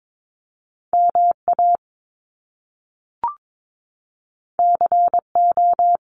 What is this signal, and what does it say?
Morse code, heard as a keyed tone, about 22 wpm, 700 Hz. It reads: MA CO
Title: US States Repeated 3x 22wpm